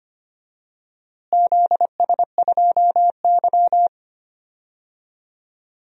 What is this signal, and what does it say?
Morse code, heard as a keyed tone, about 25 wpm, 700 Hz. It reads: ZS2Y
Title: DXCC-100 Call Signs - 4 Characters - Repeated 2x - 25wpm